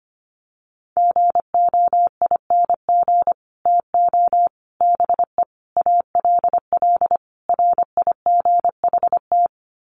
Morse code, heard as a keyed tone, about 25 wpm, 700 Hz. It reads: GOING TO BE ALL RIGHT
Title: Sets of 5 Words Encore2 25wpm